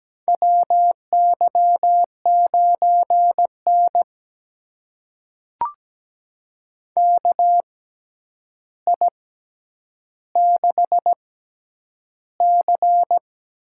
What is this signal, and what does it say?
Morse code, heard as a keyed tone, about 17 wpm, spaced as for 9 wpm, 700 Hz. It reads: WY9N KI6C
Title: USA Call Signs - 6x Character Spacing - 17wpm